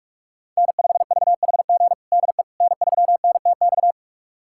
Morse code, heard as a keyed tone, about 45 wpm, 700 Hz. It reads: N5VHG DE N3NTX
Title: USA Callsign DE Callsign - 5 Characters - 45wpm